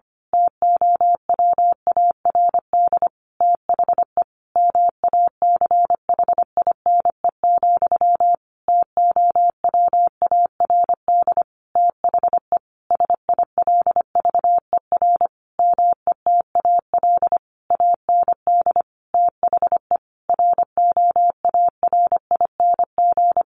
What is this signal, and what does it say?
Morse code, heard as a keyed tone, about 25 wpm, 700 Hz. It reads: TOWARD THE MACHINE, TOWARD THE SILVER METAL AND THE ROARING
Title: A Sound Of Thunder 25wpm